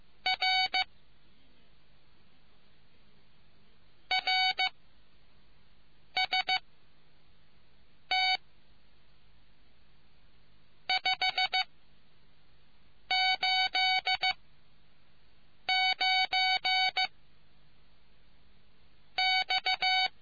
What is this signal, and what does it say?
Morse code, heard as a keyed tone, about 15 wpm, 750 Hz. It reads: R R S T 5 8 9 X